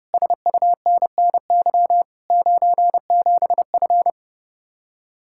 Morse code, heard as a keyed tone, about 33 wpm, 700 Hz. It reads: SUNNY 97F